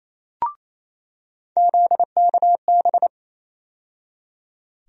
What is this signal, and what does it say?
Morse code, heard as a keyed tone, about 28 wpm, 700 Hz. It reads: ZKB
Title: USA Call Sign Postfixes - 3 Letters 28wpm